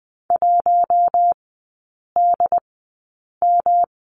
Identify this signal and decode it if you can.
Morse code, heard as a keyed tone, about 20 wpm, spaced as for 8 wpm, 700 Hz. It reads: 1DM
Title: 107 Lesson 18e - Letter m - Three Characters 20wpm